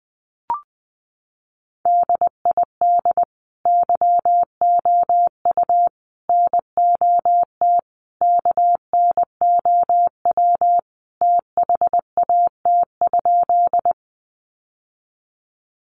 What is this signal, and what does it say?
Morse code, heard as a keyed tone, about 20 wpm, 700 Hz. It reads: DID YOU NOT KNOW THAT?
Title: Sentences from Top 600 Words 20wpm